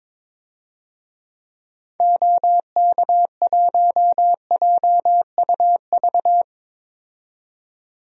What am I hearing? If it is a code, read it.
Morse code, heard as a keyed tone, about 22 wpm, 700 Hz. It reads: OK1JUV
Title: DXCC-100 Call Signs - 6 Characters - 22wpm